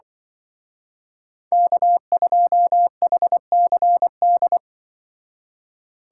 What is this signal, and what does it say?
Morse code, heard as a keyed tone, about 24 wpm, 700 Hz. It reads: K2HCD
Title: Speed Racing - USA 1x3 Call Signs 20wpm